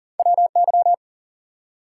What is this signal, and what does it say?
Morse code, heard as a keyed tone, about 39 wpm, 700 Hz. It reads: WY
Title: US States and Canadian Provinces 40wpm